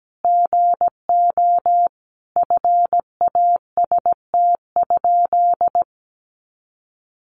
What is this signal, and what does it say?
Morse code, heard as a keyed tone, about 17 wpm, 700 Hz. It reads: GO FAST?
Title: Sentences from Top 400 Words 17wpm